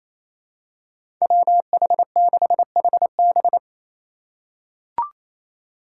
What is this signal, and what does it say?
Morse code, heard as a keyed tone, about 28 wpm, 700 Hz. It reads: WH6HB
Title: USA Call Signs - 2x Character Spacing - 28wpm